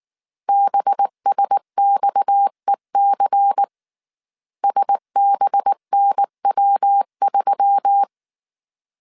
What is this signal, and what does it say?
Morse code, heard as a keyed tone, about 19 wpm, 800 Hz. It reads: BSXEC SBNW3
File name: CW / DET / 1 6 1500C S A 18wpm 2255